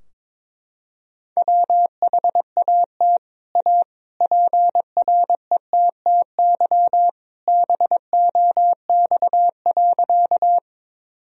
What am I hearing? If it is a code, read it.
Morse code, heard as a keyed tone, about 21 wpm, 700 Hz. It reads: WHAT A PRETTY BOX.
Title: Sentences from Top 1000 Words - Review Part 2 - 22wpm